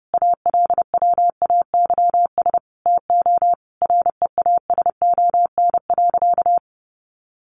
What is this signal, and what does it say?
Morse code, heard as a keyed tone, about 30 wpm, 700 Hz. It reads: ALWAYS TO REASON.
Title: Sentences from Top 1300 Words 30wpm